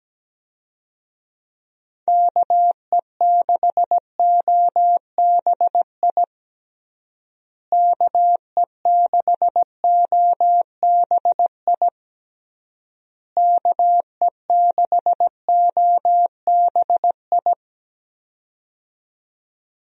Morse code, heard as a keyed tone, about 17 wpm, 700 Hz. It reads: KE6OBI KE6OBI KE6OBI
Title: USA Call Signs - 6 Characters - Repeated 3x - 17wpm